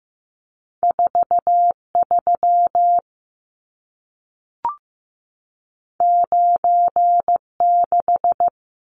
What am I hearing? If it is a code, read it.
Morse code, heard as a keyed tone, about 15 wpm, 700 Hz. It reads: 43 96
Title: Numbers - 2-Digit 15wpm